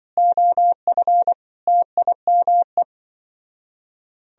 Morse code, heard as a keyed tone, about 24 wpm, 700 Hz. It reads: OF TIME
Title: Speed-Racing - Sets of 5 Words 20wpm